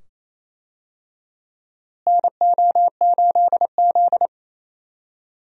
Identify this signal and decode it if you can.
Morse code, heard as a keyed tone, about 28 wpm, 700 Hz. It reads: NO8Z
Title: USA Call Signs - 4 Characters - Repeated 2x - 28wpm